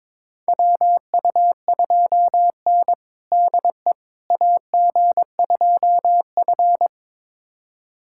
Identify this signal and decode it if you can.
Morse code, heard as a keyed tone, about 22 wpm, 700 Hz. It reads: WU2N DE AG2F
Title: USA Callsign DE Callsign - 4 Characters - 22wpm